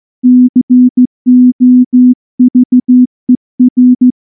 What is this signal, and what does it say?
Morse code, heard as a keyed tone, about 14 wpm, 250 Hz. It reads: COVER
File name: morse / morse-training-15WPM